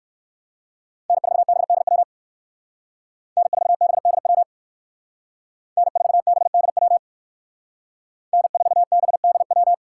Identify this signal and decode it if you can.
Morse code, heard as a keyed tone, about 68 wpm, 700 Hz. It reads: N4BDW N4BDW N4BDW N4BDW